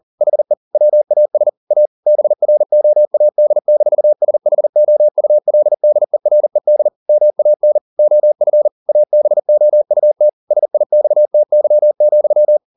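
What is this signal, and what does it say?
Morse code, heard as a keyed tone, about 40 wpm, 600 Hz. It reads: HE WAS A BROAD-SHOULDERED MAN OF ABOUT SIXTY,
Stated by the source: Morse code at 40 wpm, practice text, CW med Gnister.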